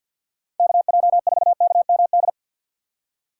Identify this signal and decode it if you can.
Morse code, heard as a keyed tone, about 50 wpm, 700 Hz. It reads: KJ4KMD
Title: USA Call Signs - 2x Character Spacing - 50wpm